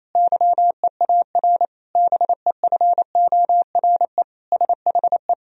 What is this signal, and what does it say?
Morse code, heard as a keyed tone, about 28 wpm, 700 Hz. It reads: YEAR BEFORE SHE
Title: Sentences from Top 1000 Words - Review Part 1 - 28wpm